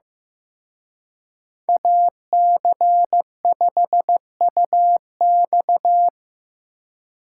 Morse code, heard as a keyed tone, about 18 wpm, 700 Hz. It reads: AC5UX
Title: Speed Racing - USA 2x2 Call Signs 15wpm